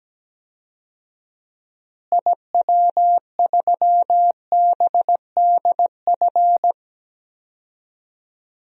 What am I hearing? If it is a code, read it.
Morse code, heard as a keyed tone, about 17 wpm, 700 Hz. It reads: IW3BDF
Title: DXCC-100 Call Signs - 6 Characters - Repeated 3x - 17wpm